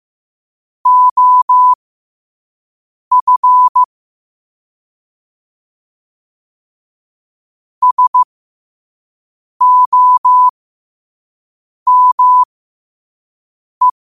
Morse code, heard as a keyed tone, about 15 wpm, spaced as for 5 wpm, 1000 Hz. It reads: OF SOME
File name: qotdfiles / 5wpm